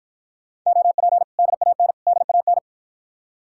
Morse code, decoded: OP DAN DAN

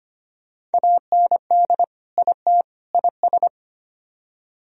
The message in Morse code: AND IT IS

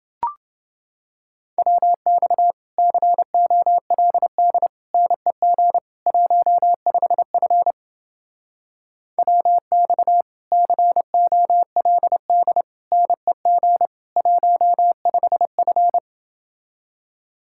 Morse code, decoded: WX COLD NEG 15F WX COLD NEG 15F

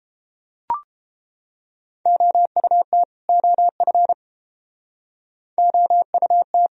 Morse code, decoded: OUT OF OUT

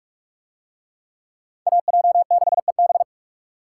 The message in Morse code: AJ6EB

